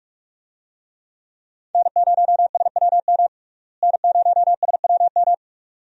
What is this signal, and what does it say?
Morse code, heard as a keyed tone, about 46 wpm, 700 Hz. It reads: N0SWM N0SWM
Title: USA Call Signs - 5 Characters - Repeated 2x - 45wpm